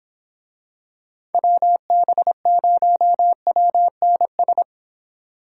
Morse code, heard as a keyed tone, about 26 wpm, 700 Hz. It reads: WB0WNS